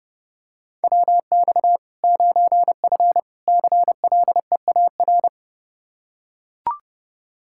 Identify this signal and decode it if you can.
Morse code, heard as a keyed tone, about 30 wpm, 700 Hz. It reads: WX 9F CLEAR
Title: K1USN SST QSO Practice 30wpm